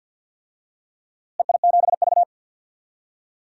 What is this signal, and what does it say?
Morse code, heard as a keyed tone, about 47 wpm, 700 Hz. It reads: EI7V
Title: DXCC-100 Call Signs - 4 Characters - Repeated 2x - 50wpm